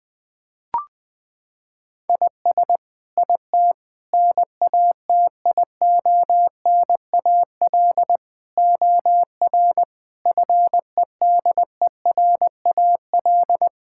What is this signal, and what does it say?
Morse code, heard as a keyed tone, about 19 wpm, 700 Hz. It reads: IS IT NATIONAL OR FEDERAL
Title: Sentences from Top 1000 Words - Review Part 5 - 20wpm